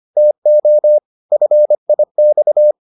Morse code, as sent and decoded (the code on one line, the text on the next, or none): - --- / ..-. .. -..-
TO FIX